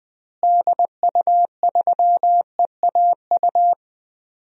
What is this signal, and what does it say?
Morse code, heard as a keyed tone, about 20 wpm, 700 Hz. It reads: DU3EAU
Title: DXCC-100 Call Signs - 6 Characters - Repeated 2x - 20wpm